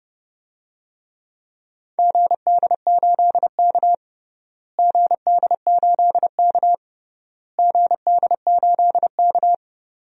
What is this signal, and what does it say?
Morse code, heard as a keyed tone, about 32 wpm, 700 Hz. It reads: GD8K GD8K GD8K
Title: DXCC-100 Call Signs - 4 Characters - Repeated 3x - 30wpm